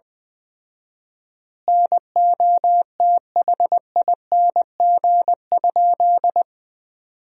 Morse code, heard as a keyed tone, about 20 wpm, 700 Hz. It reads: NOTHING?
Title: 4x Word-Spacing - Sentences from Top 500 Words - 20wpm